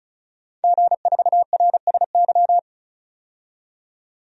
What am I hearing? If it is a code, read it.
Morse code, heard as a keyed tone, about 36 wpm, 700 Hz. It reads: G4RSY
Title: DXCC-100 Call Signs - 5 Characters - 35wpm